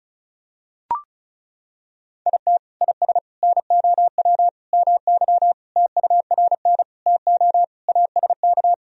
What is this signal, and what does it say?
Morse code, heard as a keyed tone, about 34 wpm, 690 Hz. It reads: IT IS NOW MY TURN TO ASK
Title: Sentences from Top 200 Words and Abbreviations 35wpm